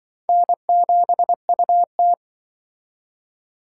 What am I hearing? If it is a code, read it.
Morse code, heard as a keyed tone, about 25 wpm, 700 Hz. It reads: N7UT
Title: Speed Racing - USA 1x2 Call Signs 20wpm